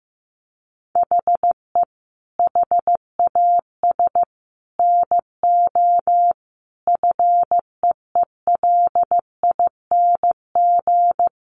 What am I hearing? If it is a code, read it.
Morse code, heard as a keyed tone, about 15 wpm, 700 Hz. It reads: HE HAS NO FEELING
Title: Sentences from Top 1200 Words 15wpm